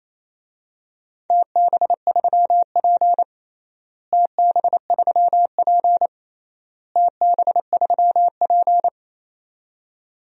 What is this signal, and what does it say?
Morse code, heard as a keyed tone, about 27 wpm, 700 Hz. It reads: TB3P TB3P TB3P
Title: DXCC-100 Call Signs - 4 Characters - Repeated 3x - 28wpm